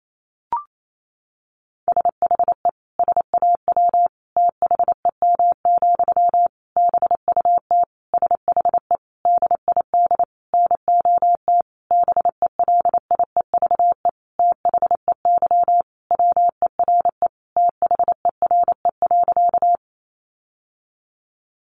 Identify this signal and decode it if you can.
Morse code, heard as a keyed tone, about 28 wpm, 700 Hz. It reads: SHE SAW THEM, BUT SHE DID NOT BELIEVE THEY WERE THERE.
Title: Sentences from Top 1000 Words 28wpm